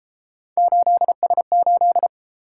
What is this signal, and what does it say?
Morse code, heard as a keyed tone, about 31 wpm, 700 Hz. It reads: 8S8